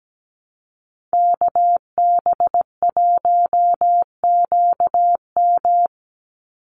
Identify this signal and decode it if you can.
Morse code, heard as a keyed tone, about 17 wpm, 700 Hz. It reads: KB1QM
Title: USA Call Signs 17wpm